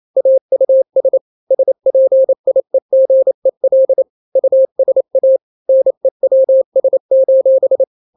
Morse code, heard as a keyed tone, about 28 wpm, 530 Hz. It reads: AUS SPIEGEL USA NEWS:
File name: daily / 27wpm